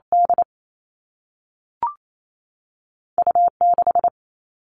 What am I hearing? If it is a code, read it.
Morse code, heard as a keyed tone, about 28 wpm, 700 Hz. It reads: D U 6
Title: Two Challenging Letter-Number Combinations 28wpm